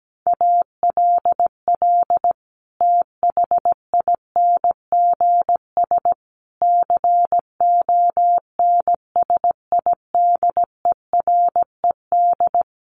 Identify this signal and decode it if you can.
Morse code, heard as a keyed tone, about 17 wpm, 700 Hz. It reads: ALL THINGS CONSIDERED
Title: Sets of 5 Words Encore 17wpm